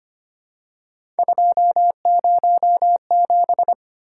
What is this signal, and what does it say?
Morse code, heard as a keyed tone, about 25 wpm, 700 Hz. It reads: 207
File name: Three Letter-Number Combinations - 25wpm